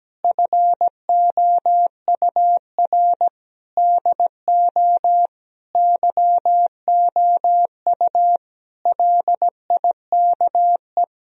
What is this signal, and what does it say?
Morse code, heard as a keyed tone, about 17 wpm, 700 Hz. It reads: FOUR DO YOU LIKE
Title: Sentences from Top 1000 Words - Review Part 1 - 17wpm